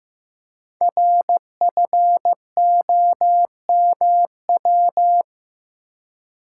Morse code, heard as a keyed tone, about 15 wpm, 700 Hz. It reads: RFOMW
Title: Code Groups - 5 Letters - 15wpm